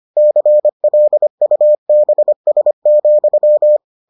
Morse code, heard as a keyed tone, about 25 wpm, 600 Hz. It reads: CLUBS,